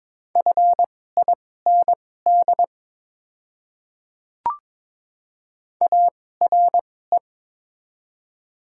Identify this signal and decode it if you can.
Morse code, heard as a keyed tone, about 23 wpm, spaced as for 16 wpm, 700 Hz. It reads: FIND ARE
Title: 2x Character Spacing - Top 100 Words - 22wpm